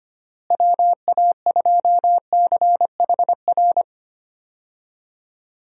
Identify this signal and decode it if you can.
Morse code, heard as a keyed tone, about 25 wpm, 700 Hz. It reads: WA2CHR
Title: USA Call Signs - 6 Characters - Repeated 2x - 25wpm